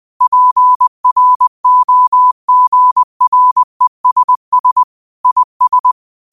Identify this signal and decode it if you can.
Morse code, heard as a keyed tone, about 20 wpm, 1000 Hz. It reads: PROGRESS IS
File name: qotdfiles / 20wpm